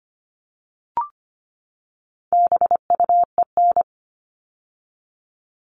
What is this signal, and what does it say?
Morse code, heard as a keyed tone, about 25 wpm, 700 Hz. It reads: BUEN
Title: All Cut Numbers - 4-Digit 25wpm